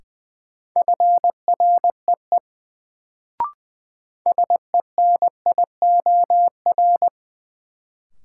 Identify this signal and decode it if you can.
Morse code, heard as a keyed tone, about 20 wpm, 700 Hz. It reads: FREE SENIOR
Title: Sets of 5 Words Encore - New Word Drill - 20wpm